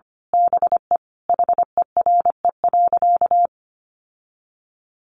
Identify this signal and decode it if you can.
Morse code, heard as a keyed tone, about 25 wpm, 700 Hz. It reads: BE HERE.
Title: Sentences from Top 1200 Words 25wpm